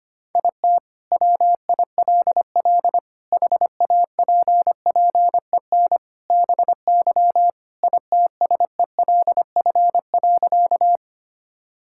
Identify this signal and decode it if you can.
Morse code, heard as a keyed tone, about 25 wpm, 700 Hz. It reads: IT WILL HAPPEN BY ITSELF.
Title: Sentences from Top 1000 Words - Review Part 5 - 25wpm